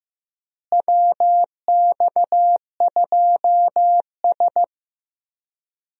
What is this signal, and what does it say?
Morse code, heard as a keyed tone, about 15 wpm, 700 Hz. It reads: WX2S
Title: USA Call Signs - 4 Characters - Repeated 3x - 15wpm